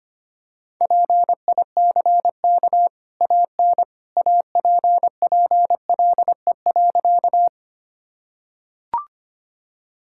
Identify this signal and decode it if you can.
Morse code, heard as a keyed tone, about 25 wpm, 700 Hz. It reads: PICK AN APPLE.
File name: Easy Sentences with Hints - 25wpm